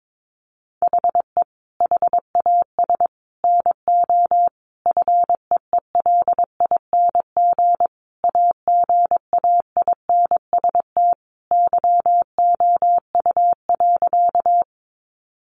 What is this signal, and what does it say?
Morse code, heard as a keyed tone, about 22 wpm, 700 Hz. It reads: HE HAS NO FEELING AGAINST YOU.
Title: Sentences from Top 1200 Words 22wpm